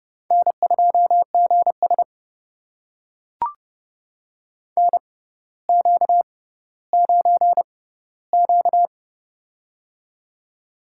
Morse code, heard as a keyed tone, about 30 wpm, spaced as for 15 wpm, 700 Hz. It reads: N2GS NQ9Q
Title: USA Call Signs - 4 Characters - 6x Character Spacing - 30wpm